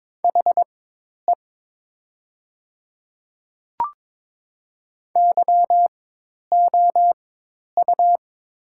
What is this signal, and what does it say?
Morse code, heard as a keyed tone, about 22 wpm, spaced as for 10 wpm, 700 Hz. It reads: HE YOU
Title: 4x Character Spacing - Top 200 Words - 22wpm